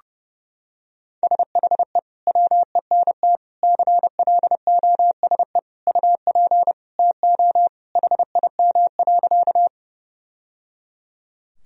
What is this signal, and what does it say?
Morse code, heard as a keyed tone, about 30 wpm, 700 Hz. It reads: SHE WENT CLOSE UP TO HIM.
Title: Sentences from Top 1000 Words - Review Part 1 - 30wpm